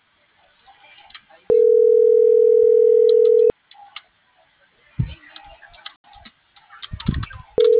Ringback tone (North America), 440 + 480 Hz.